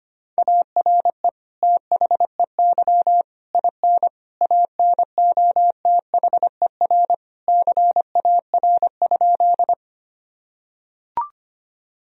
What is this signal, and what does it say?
Morse code, heard as a keyed tone, about 25 wpm, 700 Hz. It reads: ARE THEY IN ANOTHER CAR?
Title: Sentences from Top 1000 Words - Review Part 1 - 25wpm